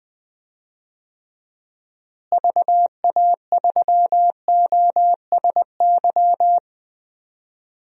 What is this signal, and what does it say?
Morse code, heard as a keyed tone, about 20 wpm, 700 Hz. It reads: VA3OSY
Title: DXCC-100 Call Signs - 6 Characters - Repeated 2x - 20wpm